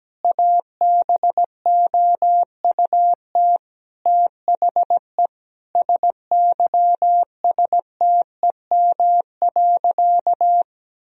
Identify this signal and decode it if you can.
Morse code, heard as a keyed tone, about 17 wpm, 700 Hz. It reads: ABOUT THE SYSTEM.
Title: Sentences from Top 1000 Words - Review Part 3 - 17wpm